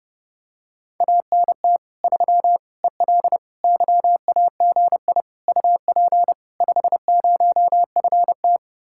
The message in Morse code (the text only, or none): ANT 3 EL YAGI UP 50FT